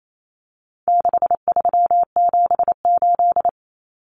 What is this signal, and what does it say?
Morse code, heard as a keyed tone, about 28 wpm, 700 Hz. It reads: 6378
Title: Numbers - 4-Digit 28wpm